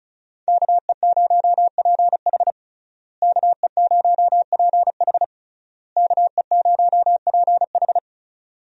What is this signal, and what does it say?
Morse code, heard as a keyed tone, about 36 wpm, 700 Hz. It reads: KE0PH KE0PH KE0PH
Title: USA Call Signs - 5 Characters - Repeated 3x - 35wpm